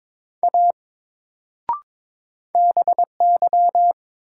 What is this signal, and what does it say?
Morse code, heard as a keyed tone, about 22 wpm, 700 Hz. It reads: A BY
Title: Sentences from Top 100 Words - New Word Drill - 22wpm